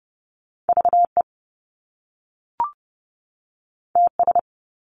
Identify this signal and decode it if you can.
Morse code, heard as a keyed tone, about 30 wpm, 700 Hz. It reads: VE TS
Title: Top Two Letter Combinations in Words 30wpm